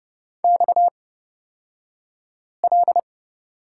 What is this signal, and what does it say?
Morse code, heard as a keyed tone, about 30 wpm, spaced as for 5 wpm, 700 Hz. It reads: XL
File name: Lesson 37d - Letter x - Two Characters 30wpm